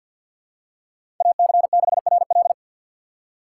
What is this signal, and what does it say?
Morse code, heard as a keyed tone, about 53 wpm, 700 Hz. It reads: AC6RL
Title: USA Call Signs - 6x Character Spacing - 50wpm